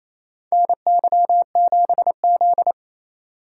Morse code, heard as a keyed tone, about 28 wpm, 700 Hz. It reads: NY7Z
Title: K1USN SST QSO Practice - 28wpm